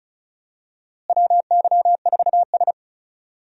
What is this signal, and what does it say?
Morse code, heard as a keyed tone, about 36 wpm, 700 Hz. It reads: WY4S
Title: USA Call Signs - 4 Characters - Repeated 2x - 35wpm